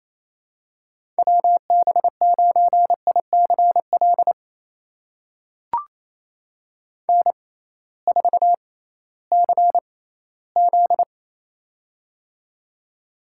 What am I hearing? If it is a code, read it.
Morse code, heard as a keyed tone, about 28 wpm, spaced as for 16 wpm, 700 Hz. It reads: WB9ICL N4CZ
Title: USA Call Signs - 6x Character Spacing - 28wpm